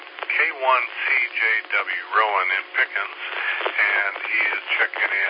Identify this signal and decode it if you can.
Emergency Alert System (SAME) tones, heard as harsh buzzing bursts, with speech. It is heard from 1.0 to 4.5 s.